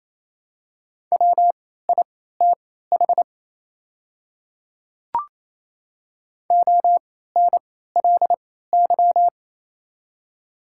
Morse code, heard as a keyed tone, about 27 wpm, spaced as for 16 wpm, 700 Hz. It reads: WITH ONLY
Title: 3x Character Spacing - Top 200 Words - 28wpm